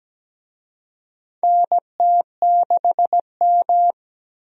Morse code, heard as a keyed tone, about 17 wpm, 700 Hz. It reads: NT6M
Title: USA Call Signs - 4 Characters - 17wpm